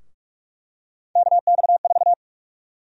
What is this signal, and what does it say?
Morse code, heard as a keyed tone, about 41 wpm, 700 Hz. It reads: KX4
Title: USA Call Sign Prefixes - Repeated 2x - 45wpm